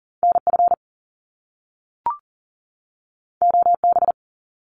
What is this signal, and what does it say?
Morse code, heard as a keyed tone, about 42 wpm, 700 Hz. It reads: NF OB
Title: Top Two Letter Combinations in Words 40wpm